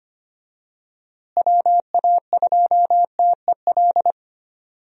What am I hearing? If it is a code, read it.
Morse code, heard as a keyed tone, about 25 wpm, 700 Hz. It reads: WA2TEL
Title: USA Call Signs - 6 Characters - Repeated 2x - 25wpm